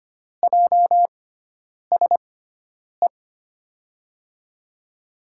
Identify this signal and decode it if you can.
Morse code, heard as a keyed tone, about 25 wpm, spaced as for 9 wpm, 700 Hz. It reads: JSE
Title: USA Call Signs - 6 Characters - 6x Character Spacing - 25wpm